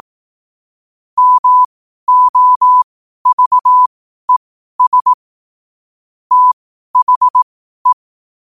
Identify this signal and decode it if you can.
Morse code, heard as a keyed tone, about 18 wpm, spaced as for 12 wpm, 1000 Hz. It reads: MOVES THE
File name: qotdfiles / 12wpm